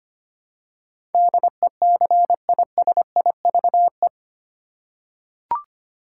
Morse code, heard as a keyed tone, about 25 wpm, 700 Hz. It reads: DECISIVE
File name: 8 Letter Words - First 3 Letters Given - 25wpm